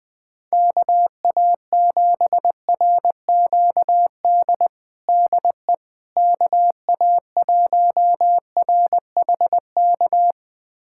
Morse code, heard as a keyed tone, about 20 wpm, 700 Hz. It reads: KA7RQD DE KA1RHK
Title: USA Callsign DE Callsign - 6 Characters - 20wpm